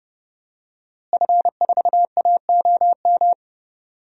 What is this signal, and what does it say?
Morse code, heard as a keyed tone, about 30 wpm, 700 Hz. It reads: F4AOM
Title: DXCC-100 Call Signs - 5 Characters - Repeated 2x - 30wpm